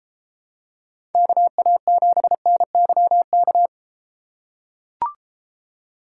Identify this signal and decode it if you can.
Morse code, heard as a keyed tone, about 33 wpm, 700 Hz. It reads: KA7NYK